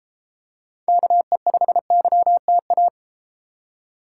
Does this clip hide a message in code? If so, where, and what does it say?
Yes: Morse code from 0.9 to 2.9 s, heard as a keyed tone, about 33 wpm, 700 Hz. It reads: KE5YTA